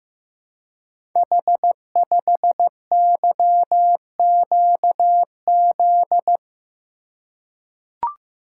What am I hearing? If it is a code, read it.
Morse code, heard as a keyed tone, about 15 wpm, 700 Hz. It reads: H5YQZ
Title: Code Groups - 5 Letter:Numbers - 15wpm